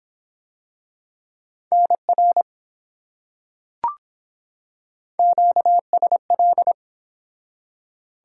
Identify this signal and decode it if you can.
Morse code, heard as a keyed tone, about 24 wpm, 700 Hz. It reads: NR QSL